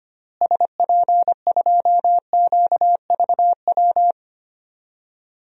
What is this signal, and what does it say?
Morse code, heard as a keyed tone, about 25 wpm, 700 Hz. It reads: SP2QVW
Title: DXCC-100 Call Signs - 6 Characters - Repeated 3x - 25wpm